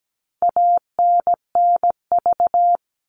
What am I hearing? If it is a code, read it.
Morse code, heard as a keyed tone, about 17 wpm, 700 Hz. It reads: ANNV
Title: All Cut Numbers - 4-Digit 17wpm